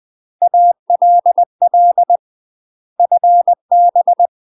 Morse code, heard as a keyed tone, about 20 wpm, 700 Hz. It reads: ALL FB
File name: UK MASTER QSOs TRAINING / 20WPM / UK MASTER QSOs 0061